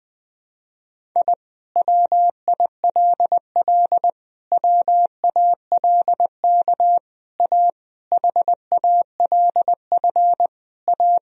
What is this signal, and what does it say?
Morse code, heard as a keyed tone, about 20 wpm, 700 Hz. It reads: I WILL WALK A HALF A
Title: Sentences from Top 1000 Words - Review Part 1 - 20wpm